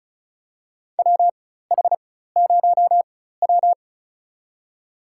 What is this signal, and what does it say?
Morse code, heard as a keyed tone, about 36 wpm, spaced as for 16 wpm, 700 Hz. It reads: WH0W
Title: USA Call Signs - 4x Character Spacing - 35wpm